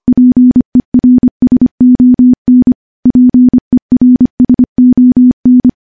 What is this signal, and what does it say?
Morse code, heard as a keyed tone, about 25 wpm, 250 Hz. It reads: PERSON PERSON
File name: morse-training-25WPM